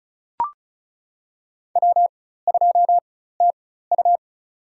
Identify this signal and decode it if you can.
Morse code, heard as a keyed tone, about 34 wpm, spaced as for 16 wpm, 700 Hz. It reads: W2TU
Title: USA Call Signs - 4 Characters - 4x Character Spacing - 35wpm